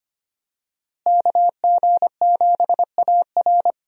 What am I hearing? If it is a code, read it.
Morse code, heard as a keyed tone, about 25 wpm, 700 Hz. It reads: KG7AR
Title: USA Call Signs - 5 Characters - Repeated 3x - 25wpm